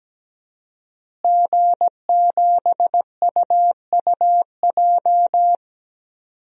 Morse code, heard as a keyed tone, about 17 wpm, 700 Hz. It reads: G7UUJ
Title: DXCC-100 Call Signs - 5 Characters - Repeated 2x - 17wpm